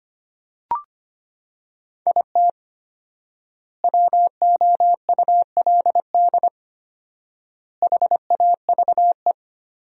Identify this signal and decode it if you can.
Morse code, heard as a keyed tone, about 25 wpm, 700 Hz. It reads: IT WOULD HAVE
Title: 4x Word-Spacing - Sentences from Top 400 Words - 25wpm